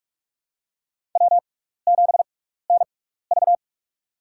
Morse code, heard as a keyed tone, about 46 wpm, spaced as for 16 wpm, 700 Hz. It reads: W7NV
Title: USA Call Signs - 4 Characters - 6x Character Spacing - 45wpm